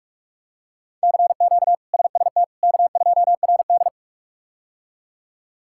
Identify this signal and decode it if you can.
Morse code, heard as a keyed tone, about 46 wpm, 700 Hz. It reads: CQ SST K2RD
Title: K1USN SST QSO Practice - 45wpm